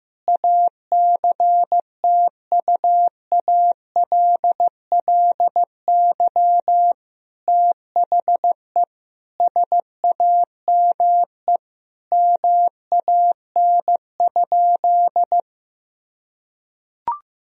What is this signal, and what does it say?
Morse code, heard as a keyed tone, about 15 wpm, 700 Hz. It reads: ACTUALLY THE SAME MAN?
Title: Sentences from Top 1000 Words - Review Part 4 - 15wpm